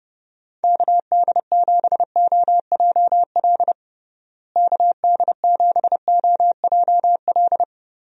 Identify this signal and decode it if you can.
Morse code, heard as a keyed tone, about 32 wpm, 700 Hz. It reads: KD7OJL KD7OJL
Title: USA Call Signs - 6 Characters - Repeated 2x - 30wpm